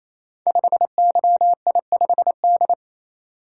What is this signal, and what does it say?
Morse code, heard as a keyed tone, about 27 wpm, 700 Hz. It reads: 5YI5D
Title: Code Groups - 5 Letter:Numbers - 28wpm